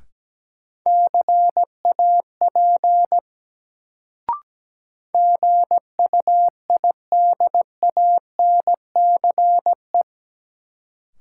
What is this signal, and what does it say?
Morse code, heard as a keyed tone, about 17 wpm, 700 Hz. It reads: CAP GUIDANCE
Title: Sentences from Top 1600 Words - New Word Drill - 17wpm